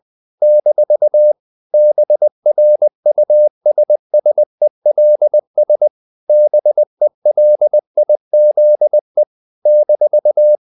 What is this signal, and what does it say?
Morse code, heard as a keyed tone, about 20 wpm, 600 Hz. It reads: - BRUSSELS BELIZE -
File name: CW / Practice / NationalCapitals20